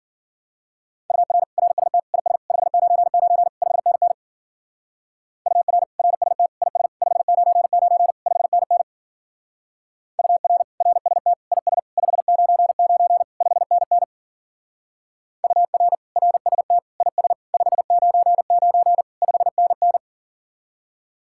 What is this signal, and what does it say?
Morse code, heard as a keyed tone, about 52 wpm, 700 Hz. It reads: UR RST IS 599 5NN UR RST IS 599 5NN UR RST IS 599 5NN UR RST IS 599 5NN